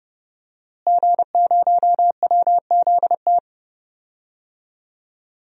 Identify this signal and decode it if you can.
Morse code, heard as a keyed tone, about 32 wpm, 700 Hz. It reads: G0WZT
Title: DXCC-100 Call Signs - 5 Characters - Repeated 2x - 30wpm